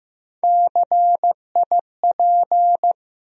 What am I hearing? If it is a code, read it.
Morse code, heard as a keyed tone, about 15 wpm, 700 Hz. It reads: CIP
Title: USA Call Sign Postfixes - 3 Letters 15wpm